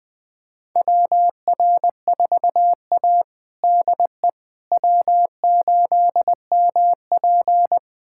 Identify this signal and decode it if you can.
Morse code, heard as a keyed tone, about 20 wpm, 700 Hz. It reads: WR4A DE W8MP
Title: USA Callsign DE Callsign - 4 Characters - 20wpm